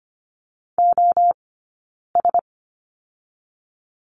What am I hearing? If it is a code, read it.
Morse code, heard as a keyed tone, about 25 wpm, spaced as for 9 wpm, 690 Hz. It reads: OS
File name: course / 25:10wpm / 092 Lesson 16d - Letter c - Two Characters 25wpm